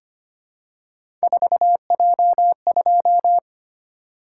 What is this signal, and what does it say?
Morse code, heard as a keyed tone, about 25 wpm, 700 Hz. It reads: 4J2
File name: Three Letter-Number Combinations - 25wpm